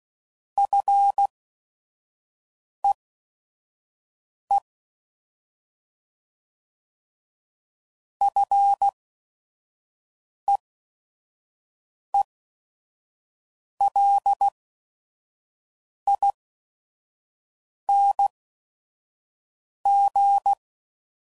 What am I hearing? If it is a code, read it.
Morse code, heard as a keyed tone, about 16 wpm, spaced as for 5 wpm, 800 Hz. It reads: FEE FEELING
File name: Mixed / Short Runs / SinglePrefixes